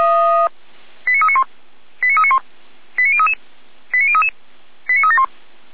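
Five-tone selective call (ZVEI), heard as quick runs of five tones, tones of about 70 ms. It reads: 89391 89391 89033 89033 89371
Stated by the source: ZVEI five-tone selective call received on an SDR.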